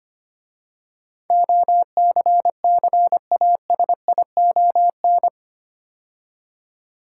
Morse code, heard as a keyed tone, about 25 wpm, 700 Hz. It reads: OCCASION
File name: 8 Letter Words - First 3 Letters Given - 25wpm